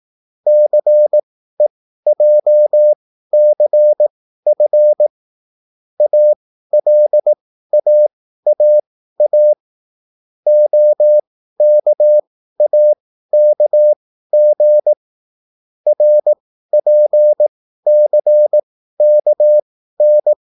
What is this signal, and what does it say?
Morse code, heard as a keyed tone, about 18 wpm, spaced as for 13 wpm, 600 Hz. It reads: CEJCF ALAAA OKAKG RPCKN
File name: QSL / CW lesson